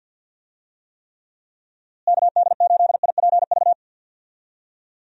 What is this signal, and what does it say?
Morse code, heard as a keyed tone, about 52 wpm, 700 Hz. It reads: KD8IPV
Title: USA Call Signs - 6 Characters - Repeated 2x - 50wpm